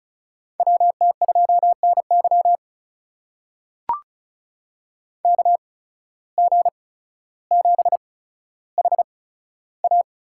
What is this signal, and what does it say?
Morse code, heard as a keyed tone, about 36 wpm, spaced as for 15 wpm, 700 Hz. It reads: WT2NY KG7HA
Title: USA Call Signs - 5 Characters - 8x Character Spacing - 35wpm